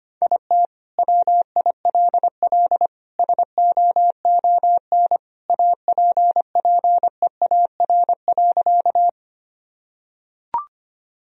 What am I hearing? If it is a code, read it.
Morse code, heard as a keyed tone, about 25 wpm, 700 Hz. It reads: IT WILL SOON APPEAR.
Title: Sentences from Top 1000 Words - Review Part 2 - 25wpm